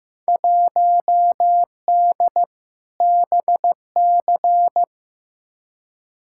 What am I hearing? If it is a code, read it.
Morse code, heard as a keyed tone, about 15 wpm, 700 Hz. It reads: 1D BC
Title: Field Day Exchanges - 15wpm